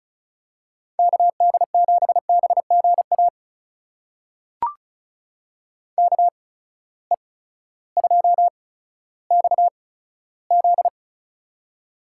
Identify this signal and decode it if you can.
Morse code, heard as a keyed tone, about 36 wpm, spaced as for 16 wpm, 700 Hz. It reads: KD7BGA KE2XZ
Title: USA Call Signs - 8x Character Spacing - 35wpm